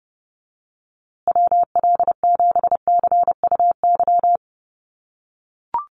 Morse code, heard as a keyed tone, about 30 wpm, 700 Hz. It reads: WL7CUY
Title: USA Call Signs Repeated 2x 30wpm